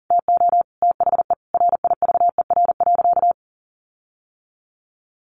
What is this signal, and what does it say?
Morse code, heard as a keyed tone, about 40 wpm, 700 Hz. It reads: TO THE RIVER.
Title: Sentences from Top 1500 Words 40wpm